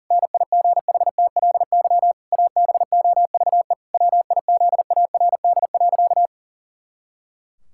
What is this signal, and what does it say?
Morse code, heard as a keyed tone, about 37 wpm, 700 Hz. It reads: NIGHTLY ABOVE WIZARD.
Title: Pangram Challenge v2 - 40wpm